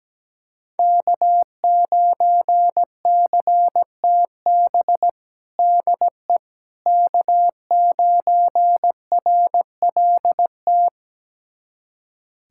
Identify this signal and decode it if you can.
Morse code, heard as a keyed tone, about 17 wpm, 700 Hz. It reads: K9CTB DE K9RLT
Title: USA Callsign DE Callsign - 5 Characters - 17wpm